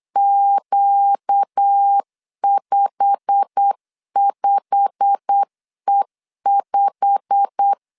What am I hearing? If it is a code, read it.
Morse code, heard as a keyed tone, about 8 wpm, 800 Hz. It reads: Q55E5
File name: LEZIONI / 8 YJ45 / 8 YJ45 8wpm 3214